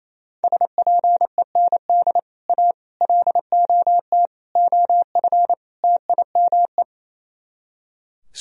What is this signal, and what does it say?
Morse code, heard as a keyed tone, about 34 wpm, 700 Hz. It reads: SPEND A LOT OF TIME
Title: Speed-Racing - Sets of 5 Words 28wpm